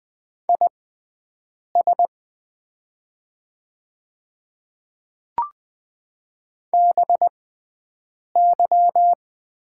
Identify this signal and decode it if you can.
Morse code, heard as a keyed tone, about 20 wpm, spaced as for 7 wpm, 700 Hz. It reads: IS BY
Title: 6x Character Spacing - Top 100 Words - 20wpm